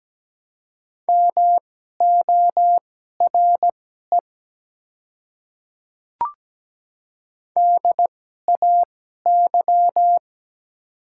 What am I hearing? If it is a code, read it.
Morse code, heard as a keyed tone, about 17 wpm, spaced as for 12 wpm, 700 Hz. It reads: MORE DAY
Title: 2x Character Spacing - Top 100 Words - 17wpm